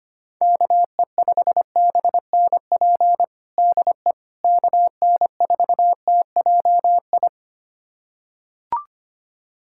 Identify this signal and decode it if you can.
Morse code, heard as a keyed tone, about 25 wpm, 700 Hz. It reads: KE5BNP DE KN4TJI
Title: USA Callsign DE Callsign 25wpm